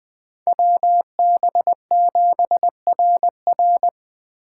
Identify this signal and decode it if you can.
Morse code, heard as a keyed tone, about 20 wpm, 700 Hz. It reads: WB7RR